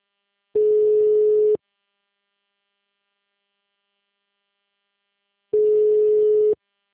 Ringback tone (Europe, 425 Hz).